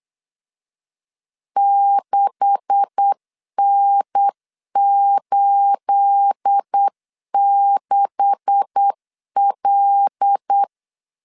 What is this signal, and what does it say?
Morse code, heard as a keyed tone, about 8 wpm, 800 Hz. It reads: 6N86L
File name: CW / DET / 1 9 1500C S A 8wpm 5637